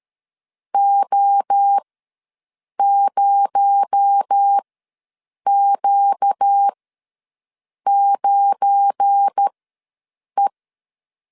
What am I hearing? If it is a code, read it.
Morse code, heard as a keyed tone, about 13 wpm, spaced as for 7 wpm, 800 Hz. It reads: O0Q9E
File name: LEZIONI / 7 GF9Z / 7 GF9 ws 12wpm 3344